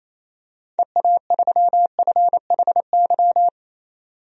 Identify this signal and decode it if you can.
Morse code, heard as a keyed tone, about 28 wpm, 700 Hz. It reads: EA3FHY
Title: DXCC-100 Call Signs - 6 Characters - 28wpm